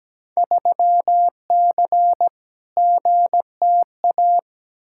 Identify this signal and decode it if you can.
Morse code, heard as a keyed tone, about 17 wpm, 700 Hz. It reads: 3C GTA
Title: Field Day Exchanges Encore - 17wpm